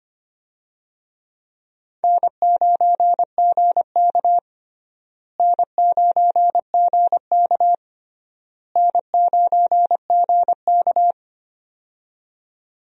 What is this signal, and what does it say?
Morse code, heard as a keyed tone, about 25 wpm, 700 Hz. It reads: N9GK N9GK N9GK
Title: USA Call Signs - 4 Characters - Repeated 3x - 25wpm